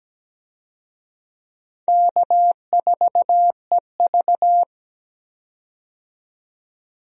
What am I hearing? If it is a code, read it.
Morse code, heard as a keyed tone, about 17 wpm, 700 Hz. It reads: K4EV
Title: USA Call Signs - 4 Characters - 17wpm